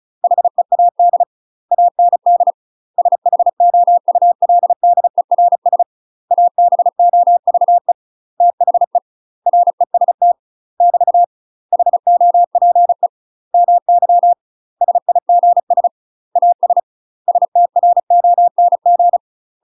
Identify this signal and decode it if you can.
Morse code, heard as a keyed tone, about 36 wpm, 700 Hz. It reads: HEAD AND SHOULDERS ABOVE THE REST <BT> HOPE MY SIGS AS STRONG
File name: UK MASTER QSOs TRAINING / 35WPM / UK MASTER QSOs 0018